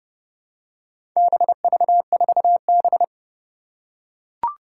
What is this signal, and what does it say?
Morse code, heard as a keyed tone, about 28 wpm, 700 Hz. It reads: BV4B
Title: Four Challenging Letter-Number Combinations 30wpm